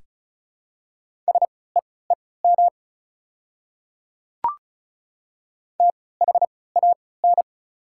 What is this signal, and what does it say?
Morse code, heard as a keyed tone, about 36 wpm, spaced as for 20 wpm, 690 Hz. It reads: SEEM THAN
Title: 3x Character Spacing - Top 500 Words - 35wpm